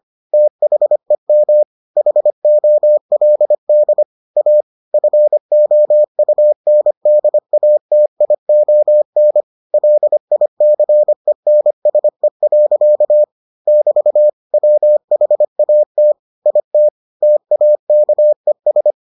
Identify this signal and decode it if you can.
Morse code, heard as a keyed tone, about 25 wpm, 600 Hz. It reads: THEM HOLD A FOUNDATION LICENSE. <BT> WHAT IT TAKES